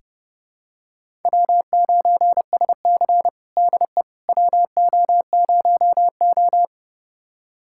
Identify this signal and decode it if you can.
Morse code, heard as a keyed tone, about 30 wpm, 700 Hz. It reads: W9SC DE WO0O
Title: USA Callsign DE Callsign - 4 Characters - 30wpm